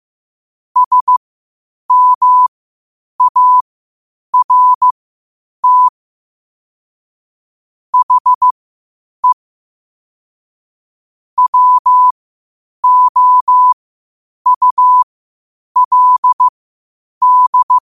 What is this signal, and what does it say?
Morse code, heard as a keyed tone, about 15 wpm, spaced as for 8 wpm, 1000 Hz. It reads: SMART HE WOULD
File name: qotdfiles / 8wpm